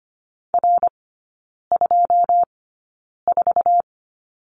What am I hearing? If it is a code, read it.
Morse code, heard as a keyed tone, about 25 wpm, spaced as for 9 wpm, 700 Hz. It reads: R24
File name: Lesson 14e - Number 2 - Three Characters 25wpm